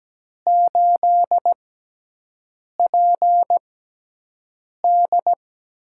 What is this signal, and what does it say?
Morse code, heard as a keyed tone, about 17 wpm, spaced as for 6 wpm, 700 Hz. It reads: 8PD
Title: USA Call Signs - 5 Characters - 6x Character Spacing - 17wpm